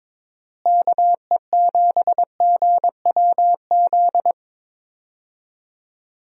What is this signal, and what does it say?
Morse code, heard as a keyed tone, about 22 wpm, 700 Hz. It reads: KE7GWZ
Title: USA Call Signs - 6 Characters - Repeated 3x - 22wpm